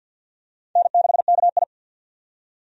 Morse code, heard as a keyed tone, about 52 wpm, 700 Hz. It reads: N6KI
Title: USA Call Signs - 4 Characters - 50wpm